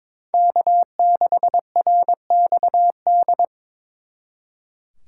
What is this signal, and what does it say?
Morse code, heard as a keyed tone, about 26 wpm, 700 Hz. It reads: K6RXD
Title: Speed Racing - USA 1x3 Call Signs 22wpm